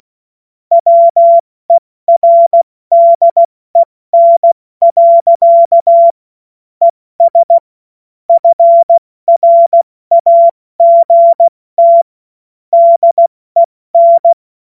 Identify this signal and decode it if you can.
Morse code, heard as a keyed tone, about 16 wpm, 680 Hz. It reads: WERDEN. ES FRAGT DEN
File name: Morsekurs / Witze / Witz 28.November 2023 70 F80